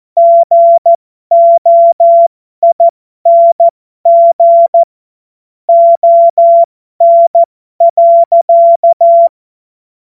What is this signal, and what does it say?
Morse code, heard as a keyed tone, about 14 wpm, spaced as for 12 wpm, 680 Hz. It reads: GOING ON.
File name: Morsekurs / Witze